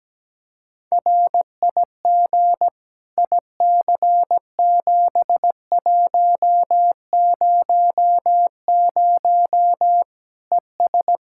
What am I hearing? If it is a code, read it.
Morse code, heard as a keyed tone, about 17 wpm, 700 Hz. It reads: RIG IC7100 ES
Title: Practice QSO - Exchange 2 - Info - 17wpm